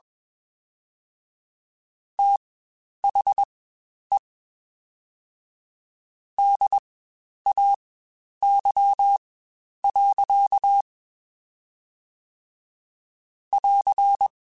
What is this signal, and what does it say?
Morse code, heard as a keyed tone, about 21 wpm, spaced as for 10 wpm, 800 Hz. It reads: THE DAY. <AR>
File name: Spaced Characters / Spaced Characters-mp3s / Weather 5 12 sp @20